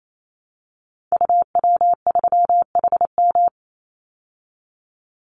Morse code, heard as a keyed tone, about 28 wpm, 690 Hz. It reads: UW3HM
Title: Call Signs Repeated 3x - 28wpm